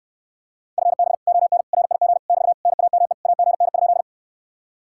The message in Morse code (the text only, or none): FB ON RIG <BT> NAME AGN?